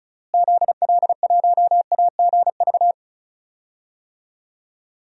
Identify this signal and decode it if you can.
Morse code, heard as a keyed tone, about 34 wpm, 700 Hz. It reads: ZL1AGV
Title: DXCC-100 Call Signs - 6 Characters - Repeated 3x - 35wpm